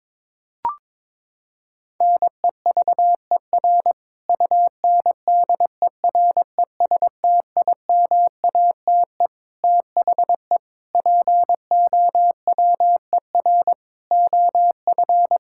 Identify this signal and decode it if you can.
Morse code, heard as a keyed tone, about 22 wpm, 700 Hz. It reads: NEVER UNDERESTIMATE THE POWER OF
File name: practice / 22wpm / Sentences with 4-Word Phrases - 22wpm